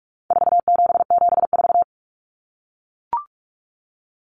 Morse code, heard as a keyed tone, about 44 wpm, 700 Hz. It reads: 4774
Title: Common Cut Numbers - 4-Digit 45wpm